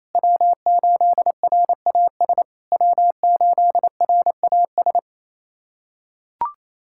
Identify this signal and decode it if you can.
Morse code, heard as a keyed tone, about 28 wpm, 700 Hz. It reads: W8RAS W8RAS
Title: SOTA QSO Practice 28wpm